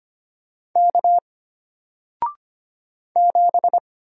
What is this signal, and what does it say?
Morse code, heard as a keyed tone, about 25 wpm, 700 Hz. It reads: K 7
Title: Single Character (Letter:Number:Punctuation:Prosign) Alt 25wpm